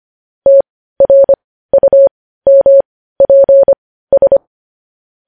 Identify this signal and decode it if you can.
Morse code, heard as a keyed tone, about 25 wpm, spaced as for 15 wpm, 550 Hz. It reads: TRUMPS
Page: news headlines in morse code at 15 wpm